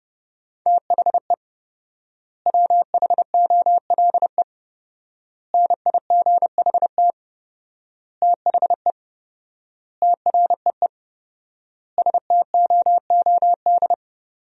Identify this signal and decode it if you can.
Morse code, heard as a keyed tone, about 30 wpm, 700 Hz. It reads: THE WHOLE NIGHT THE TREE STOOD
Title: 4x Word-Spacing - Sentences from Top 500 Words - 30wpm